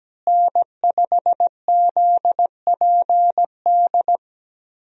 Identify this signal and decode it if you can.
Morse code, heard as a keyed tone, about 17 wpm, 700 Hz. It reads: N5ZPD
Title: USA Call Signs - 5 Characters - Repeated 2x - 17wpm